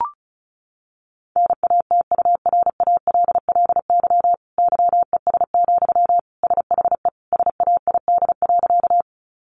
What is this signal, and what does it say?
Morse code, heard as a keyed tone, about 34 wpm, 700 Hz. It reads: NATURALLY YES, SHE SAID.
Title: Sentences from Top 1300 Words 35wpm